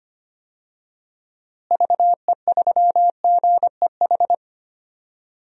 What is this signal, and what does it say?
Morse code, heard as a keyed tone, about 25 wpm, 700 Hz. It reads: VE3GEH
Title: DXCC-100 Call Signs - 6 Characters - Repeated 3x - 25wpm